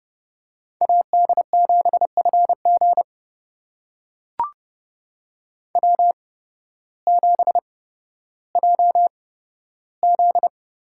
Morse code, heard as a keyed tone, about 30 wpm, spaced as for 14 wpm, 700 Hz. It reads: AD7FG W7JZ
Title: USA Call Signs - 5 Characters - 8x Character Spacing - 30wpm